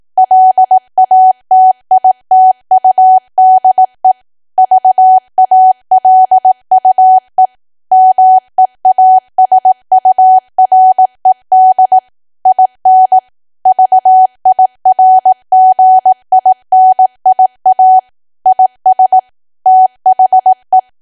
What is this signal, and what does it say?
Morse code, heard as a keyed tone, about 18 wpm, 750 Hz. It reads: LATITUDE VALUE MEASURED IN VIRGINIA IS THE
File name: cwa / W1AW-Level2 / 18WPM